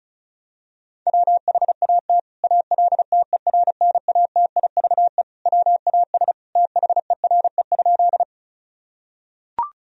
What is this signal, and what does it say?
Morse code, heard as a keyed tone, about 36 wpm, 700 Hz. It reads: WHAT ALTERNATIVE WAS THERE?
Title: Sentences from Top 1000 Words - Review Part 4 - 35wpm